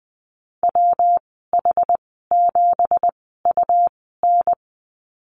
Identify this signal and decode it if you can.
Morse code, heard as a keyed tone, about 20 wpm, spaced as for 14 wpm, 700 Hz. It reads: WH7UN
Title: 213 Lesson 34g - Number 8 - Call Signs 20wpm